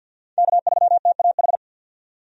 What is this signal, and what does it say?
Morse code, heard as a keyed tone, about 52 wpm, 700 Hz. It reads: K3TAH
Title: USA Call Signs - 5 Characters - 50wpm